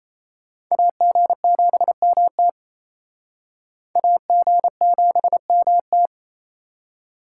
Morse code, heard as a keyed tone, about 38 wpm, 700 Hz. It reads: AG7MT AG7MT